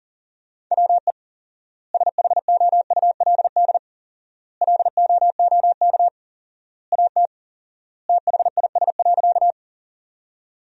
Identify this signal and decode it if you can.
Morse code, heard as a keyed tone, about 40 wpm, 700 Hz. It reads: WE SHOULD LOOK AT THIS.
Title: 4x Word-Spacing - Sentences from Top 300 Words - 40wpm